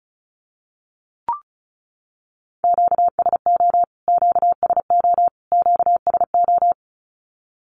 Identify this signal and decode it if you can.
Morse code, heard as a keyed tone, about 34 wpm, 700 Hz. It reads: QSO QSO QSO
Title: Q-Codes Repeated 3x 35wpm